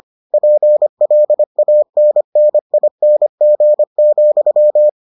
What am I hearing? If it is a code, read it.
Morse code, heard as a keyed tone, about 25 wpm, 600 Hz. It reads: PLANNING,